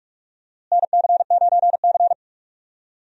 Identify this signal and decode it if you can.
Morse code, heard as a keyed tone, about 46 wpm, 700 Hz. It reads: NC9C
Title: USA Call Signs - 4 Characters - Repeated 3x - 45wpm